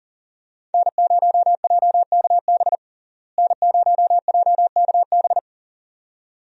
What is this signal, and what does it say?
Morse code, heard as a keyed tone, about 40 wpm, 700 Hz. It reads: N0JKB N0JKB
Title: USA Call Signs - 5 Characters - Repeated 2x - 40wpm